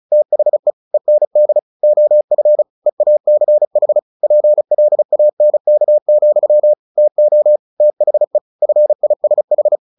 Morse code, heard as a keyed tone, about 34 wpm, 600 Hz. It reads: THE END OF EACH PLANK, TO THE FISH
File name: the old man and the sea / the old man and the sea 1 35wpm